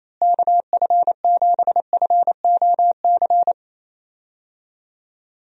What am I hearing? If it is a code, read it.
Morse code, heard as a keyed tone, about 28 wpm, 700 Hz. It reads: KF7FOC
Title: USA Call Signs - 6 Characters - Repeated 2x - 28wpm